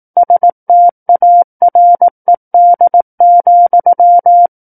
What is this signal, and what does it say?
Morse code, heard as a keyed tone, about 18 wpm, 700 Hz. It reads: STARED,
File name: 18 WPM